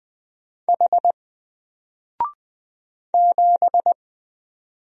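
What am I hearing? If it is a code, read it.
Morse code, heard as a keyed tone, about 20 wpm, 700 Hz. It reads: H 7
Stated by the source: Single letter and number Morse drill at 20 wpm, from N4LCD.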